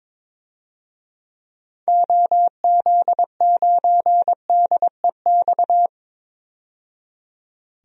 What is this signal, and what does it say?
Morse code, heard as a keyed tone, about 22 wpm, 700 Hz. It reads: OZ9DEX
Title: DXCC-100 Call Signs - 6 Characters - 22wpm